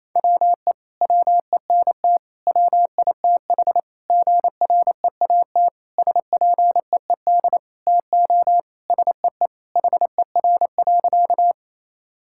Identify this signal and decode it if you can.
Morse code, heard as a keyed tone, about 28 wpm, 700 Hz. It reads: WE WENT WITH GREAT SPEED TO SEE HER.
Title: Sentences from Top 1000 Words - Review Part 5 - 28wpm